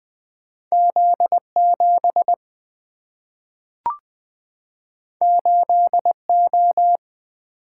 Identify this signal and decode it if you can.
Morse code, heard as a keyed tone, about 20 wpm, 700 Hz. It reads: Z7 8O
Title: Single-Letter Single-Number 20wpm